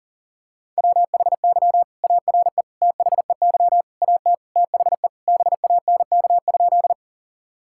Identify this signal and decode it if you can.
Morse code, heard as a keyed tone, about 40 wpm, 690 Hz. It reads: WHY ARE THEY AT THE BANK?
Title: Sentences from Top 1000 Words - Review Part 6 - 40wpm